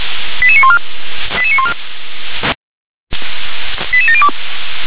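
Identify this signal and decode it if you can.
Five-tone selective call (ZVEI), heard as quick runs of five tones, tones of about 70 ms. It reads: 88914 88914 88702
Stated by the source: Five-tone ZVEI selective call 88730 received on an SDR.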